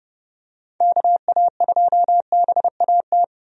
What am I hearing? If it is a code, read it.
Morse code, heard as a keyed tone, about 30 wpm, 700 Hz. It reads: KA2BAT
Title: USA Call Signs - 6 Characters - 30wpm